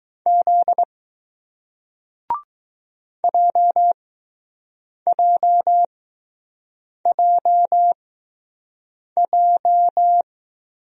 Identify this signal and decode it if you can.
Morse code, heard as a keyed tone, about 20 wpm, spaced as for 6 wpm, 700 Hz. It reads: Z JJJJ